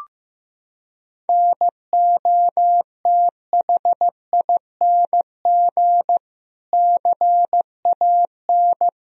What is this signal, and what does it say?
Morse code, heard as a keyed tone, about 16 wpm, 700 Hz. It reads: NOTHING CAN
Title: Sentences from Top 1000 Words - Review Part 2 - 15wpm